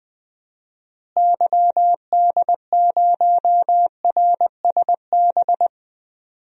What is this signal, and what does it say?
Morse code, heard as a keyed tone, about 20 wpm, 700 Hz. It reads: YD0RSB
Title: DXCC-100 Call Signs - 6 Characters - Repeated 3x - 20wpm